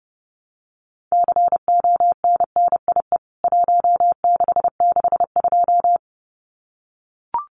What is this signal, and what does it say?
Morse code, heard as a keyed tone, about 30 wpm, 700 Hz. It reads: CONNIE 1662
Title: CWT Exchange 30wpm